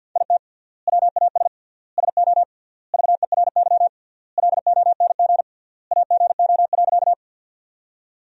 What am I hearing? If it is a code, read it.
Morse code, heard as a keyed tone, about 53 wpm, 700 Hz. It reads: IT WAS SO VERY LONG AGO.
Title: 3x Word-Spacing - Sentences from Top 500 Words - 50wpm